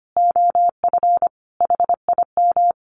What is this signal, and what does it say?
Morse code, heard as a keyed tone, about 25 wpm, 700 Hz. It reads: OF HIM
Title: Sentences from Top 1300 Words 25wpm